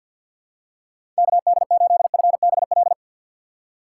Morse code, heard as a keyed tone, about 51 wpm, 700 Hz. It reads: KD8FBL
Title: USA Call Signs - 3x Character Spacing - 50wpm